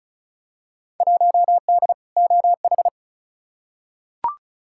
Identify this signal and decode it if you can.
Morse code, heard as a keyed tone, about 36 wpm, 700 Hz. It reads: 1D OH
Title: Field Day Exchanges - 35wpm